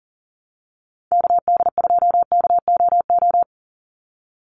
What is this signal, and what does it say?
Morse code, heard as a keyed tone, about 40 wpm, 700 Hz. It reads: KD2KOO
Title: USA Call Signs 40wpm